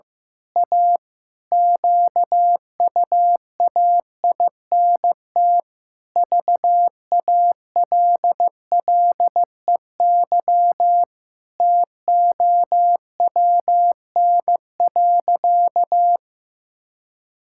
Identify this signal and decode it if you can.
Morse code, heard as a keyed tone, about 15 wpm, 700 Hz. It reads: A QUAINT VALLEY TOWN.
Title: Pangram Challenge v2 - 15wpm